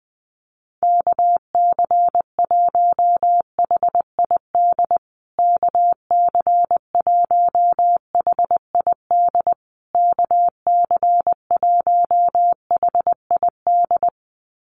Morse code, heard as a keyed tone, about 20 wpm, 700 Hz. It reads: KC1HID KC1HID KC1HID
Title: USA Call Signs Repeated 3x 20wpm